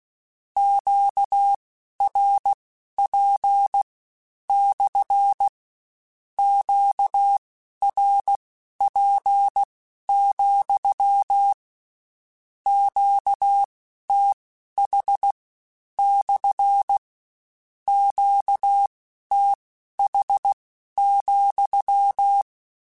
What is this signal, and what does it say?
Morse code, heard as a keyed tone, about 16 wpm, spaced as for 11 wpm, 800 Hz. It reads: QRP / QRP, QTH / QTH,
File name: Q Calls / Q Calls-mp3s / Q Calls SPSL2 @15